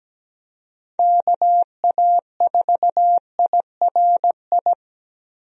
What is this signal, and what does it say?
Morse code, heard as a keyed tone, about 17 wpm, 700 Hz. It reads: KA4IRI
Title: USA Call Signs - 6 Characters - Repeated 2x - 17wpm